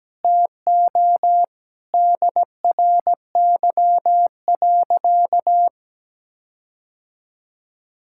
Morse code, heard as a keyed tone, about 17 wpm, 700 Hz. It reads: TO DRY.
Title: Sentences from Top 1600 Words 17wpm